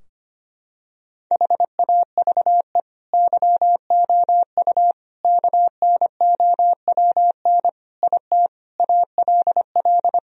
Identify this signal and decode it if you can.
Morse code, heard as a keyed tone, about 25 wpm, 700 Hz. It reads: HAVE YOU KNOWN IT ALL
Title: Sentences from Top 1000 Words - Review Part 3 - 25wpm